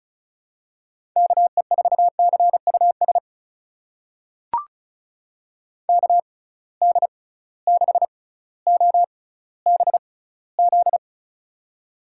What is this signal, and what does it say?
Morse code, heard as a keyed tone, about 34 wpm, spaced as for 18 wpm, 700 Hz. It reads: KE4CUS KD6OBZ
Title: USA Call Signs - 6 Characters - 6x Character Spacing - 35wpm